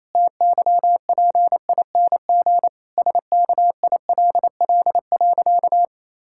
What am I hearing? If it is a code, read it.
Morse code, heard as a keyed tone, about 28 wpm, 700 Hz. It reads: TYPING SKILL.
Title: Pangram Challenge v2 - 28wpm